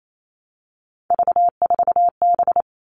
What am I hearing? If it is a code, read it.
Morse code, heard as a keyed tone, about 28 wpm, 700 Hz. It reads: V4B
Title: All Cut Numbers - 3-Digit 28wpm